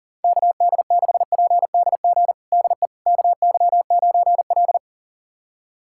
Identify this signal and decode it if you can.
Morse code, heard as a keyed tone, about 40 wpm, 700 Hz. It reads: KD6PDG DE KY9L
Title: USA Callsign DE Callsign 40wpm